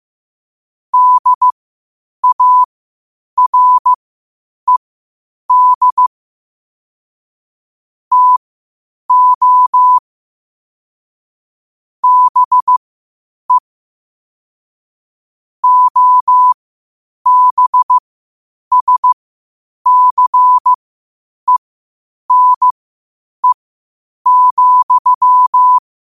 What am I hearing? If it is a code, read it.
Morse code, heard as a keyed tone, about 15 wpm, spaced as for 8 wpm, 1000 Hz. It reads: DARED TO BE OBSCENE,
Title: 8 WPM morse code quotes for Fri, 01 Mar 2024